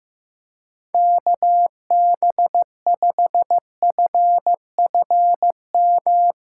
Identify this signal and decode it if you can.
Morse code, heard as a keyed tone, about 16 wpm, 700 Hz. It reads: KB5FFM
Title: USA Call Signs - 6 Characters - 4x Character Spacing - 15wpm